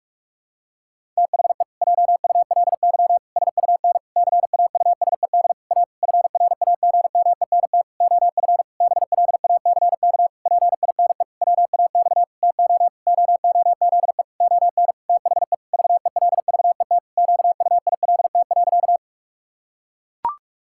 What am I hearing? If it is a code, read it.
Morse code, heard as a keyed tone, about 41 wpm, 700 Hz. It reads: THE JULY SUN CAUSED A FRAGMENT OF BLACK PINE WAX TO OOZE ON THE VELVET QUILT.
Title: Pangram Challenge v2 - 45wpm